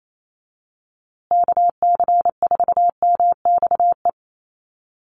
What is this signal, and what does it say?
Morse code, heard as a keyed tone, about 28 wpm, 700 Hz. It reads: KC4MXE
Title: USA Call Signs Repeated 3x 28wpm